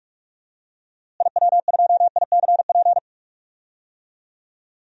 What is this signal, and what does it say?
Morse code, heard as a keyed tone, about 45 wpm, 700 Hz. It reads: IW2ICP
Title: DXCC-100 Call Signs - 6 Characters - Repeated 2x - 45wpm